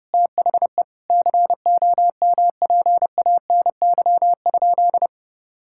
Morse code, heard as a keyed tone, about 30 wpm, 700 Hz. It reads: THE COMPANY?
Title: Sentences from Top 1600 Words 30wpm